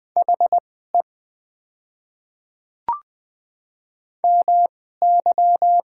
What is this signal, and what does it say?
Morse code, heard as a keyed tone, about 20 wpm, spaced as for 14 wpm, 700 Hz. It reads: HE MY
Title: 2x Character Spacing - Top 300 Words - 20wpm